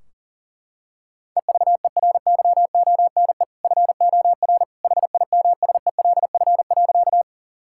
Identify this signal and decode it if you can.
Morse code, heard as a keyed tone, about 40 wpm, 700 Hz. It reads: EVERYONE FOR HIMSELF.
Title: Sentences from Top 1600 Words 40wpm